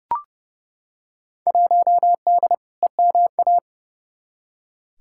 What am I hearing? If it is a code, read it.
Morse code, heard as a keyed tone, about 30 wpm, 700 Hz. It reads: 1D EMA
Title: Field Day Exchanges - 30wpm